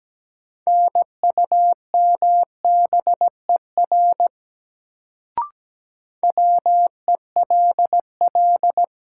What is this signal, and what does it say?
Morse code, heard as a keyed tone, about 17 wpm, 700 Hz. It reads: NUMBER WELL
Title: Sets of 2 Words - New Word Drill - 17wpm